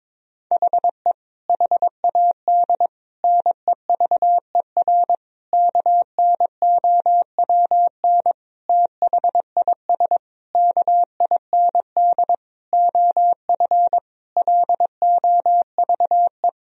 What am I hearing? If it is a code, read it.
Morse code, heard as a keyed tone, about 22 wpm, 700 Hz. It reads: HE HAD NEVER KNOWN THIS KIND OF LOVE
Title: Sentences from Top 1000 Words - Review Part 3 - 22wpm